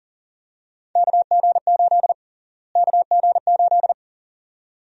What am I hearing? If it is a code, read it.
Morse code, heard as a keyed tone, about 40 wpm, 700 Hz. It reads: KG8 KG8
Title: USA Call Sign Prefixes - Repeated 2x - 40wpm